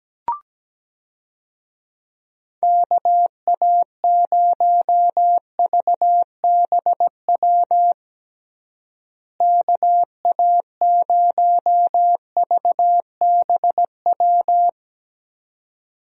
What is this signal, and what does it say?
Morse code, heard as a keyed tone, about 17 wpm, 700 Hz. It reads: KA0VBW KA0VBW
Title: USA Call Signs - 6 Characters - Repeated 2x - 17wpm